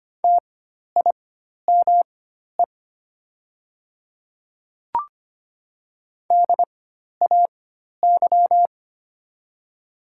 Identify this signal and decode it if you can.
Morse code, heard as a keyed tone, about 25 wpm, spaced as for 12 wpm, 700 Hz. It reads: TIME DAY
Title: 4x Character Spacing - Top 100 Words - 25wpm